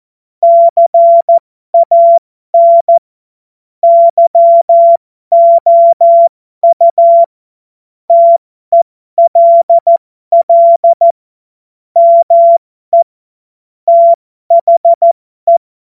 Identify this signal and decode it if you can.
Morse code, heard as a keyed tone, about 14 wpm, spaced as for 12 wpm, 680 Hz. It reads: CAN YOU TELL ME THE
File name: Morsekurs / Witze